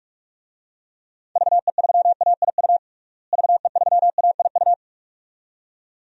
Morse code, heard as a keyed tone, about 44 wpm, 700 Hz. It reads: VE3AIU VE3AIU
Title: DXCC-100 Call Signs - 6 Characters - Repeated 2x - 45wpm